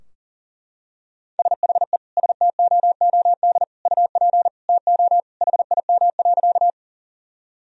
Morse code, heard as a keyed tone, about 40 wpm, 700 Hz. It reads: SHE STOOD UP TO HIM.
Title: Sentences from Top 1000 Words - Review Part 2 - 40wpm